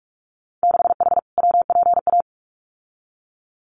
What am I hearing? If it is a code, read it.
Morse code, heard as a keyed tone, about 46 wpm, 700 Hz. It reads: 6H WPA
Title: Winter Field Day Exchanges Encore 45wpm